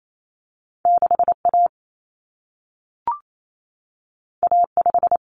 Morse code, heard as a keyed tone, about 28 wpm, 700 Hz. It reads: 6A A5
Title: Common Cut Numbers - 2-Digit 28wpm